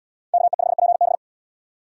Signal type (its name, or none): Morse code